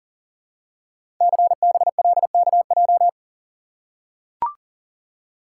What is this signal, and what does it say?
Morse code, heard as a keyed tone, about 40 wpm, 700 Hz. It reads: CBLKJ
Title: Code Groups - 5 Letters - 40wpm